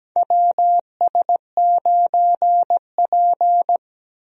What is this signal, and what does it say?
Morse code, heard as a keyed tone, about 17 wpm, 700 Hz. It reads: WS9P
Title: USA Call Signs - 4 Characters - Repeated 2x - 17wpm